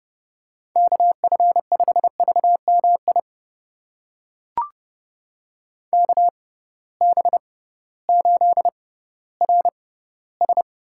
Morse code, heard as a keyed tone, about 30 wpm, spaced as for 16 wpm, 700 Hz. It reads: KF5VMI KB8RS
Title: USA Call Signs - 6 Characters - 6x Character Spacing - 30wpm